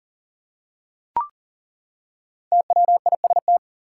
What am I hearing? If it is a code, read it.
Morse code, heard as a keyed tone, about 40 wpm, 700 Hz. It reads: TWIST
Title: Five Letter Words - 40wpm